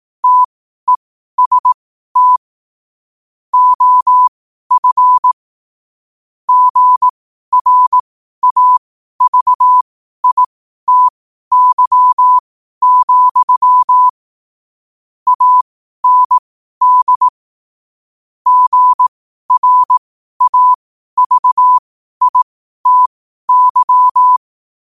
Morse code, heard as a keyed tone, about 18 wpm, spaced as for 12 wpm, 1000 Hz. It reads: TEST OF GRAVITY, AND GRAVITY
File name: qotdfiles / 12wpm